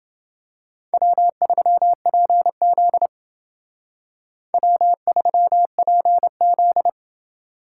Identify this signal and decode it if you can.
Morse code, heard as a keyed tone, about 28 wpm, 700 Hz. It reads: W3PZ W3PZ